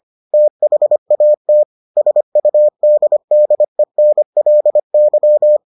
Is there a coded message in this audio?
Yes: Morse code from 0.3 to 5.6 s, heard as a keyed tone, about 25 wpm, 600 Hz. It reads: THAT SUDDENLY